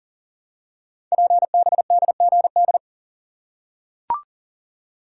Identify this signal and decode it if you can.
Morse code, heard as a keyed tone, about 39 wpm, 690 Hz. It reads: PBDGD
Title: Code Groups - 5 Letters - 40wpm